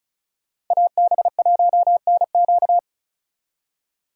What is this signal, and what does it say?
Morse code, heard as a keyed tone, about 45 wpm, 700 Hz. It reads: AB1NQ